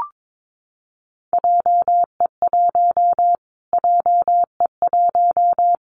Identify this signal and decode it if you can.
Morse code, heard as a keyed tone, about 22 wpm, 700 Hz. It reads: JE1 JE1
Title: Callsign Prefixes for Top 25 DXCC - Repeated 3x - 22wpm